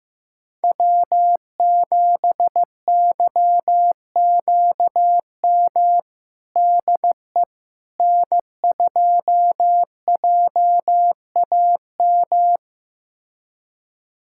Morse code, heard as a keyed tone, about 15 wpm, 700 Hz. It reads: W7YQM DE N2JAM
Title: USA Callsign DE Callsign - 5 Characters - 15wpm